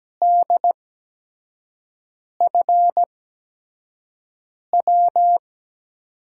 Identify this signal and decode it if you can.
Morse code, heard as a keyed tone, about 17 wpm, spaced as for 5 wpm, 700 Hz. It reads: DFW
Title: USA Call Signs - 8x Character Spacing - 17wpm